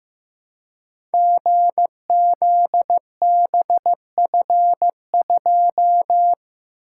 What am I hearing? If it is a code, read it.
Morse code, heard as a keyed tone, about 15 wpm, 700 Hz. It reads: GZBF2
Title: Code Groups - 5 Letter:Numbers - 15wpm